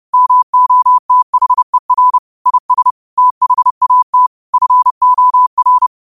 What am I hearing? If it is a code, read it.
Morse code, heard as a keyed tone, about 30 wpm, 1000 Hz. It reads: MOTHER IS THAT FOR